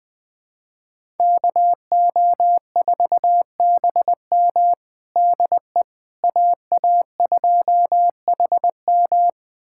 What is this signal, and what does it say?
Morse code, heard as a keyed tone, about 21 wpm, 700 Hz. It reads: KO4BM DE AA2HM
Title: USA Callsign DE Callsign - 5 Characters - 20wpm